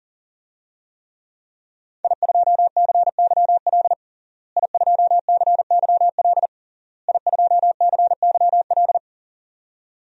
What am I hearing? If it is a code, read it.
Morse code, heard as a keyed tone, about 40 wpm, 700 Hz. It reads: I2CYL I2CYL I2CYL
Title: DXCC-100 Call Signs - 5 Characters - Repeated 3x - 40wpm